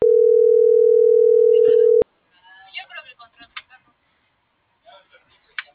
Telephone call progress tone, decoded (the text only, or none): ringback tone